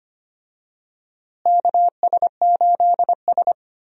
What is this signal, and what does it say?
Morse code, heard as a keyed tone, about 25 wpm, 700 Hz. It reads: KS8S
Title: USA Call Signs - 4 Characters - Repeated 3x - 25wpm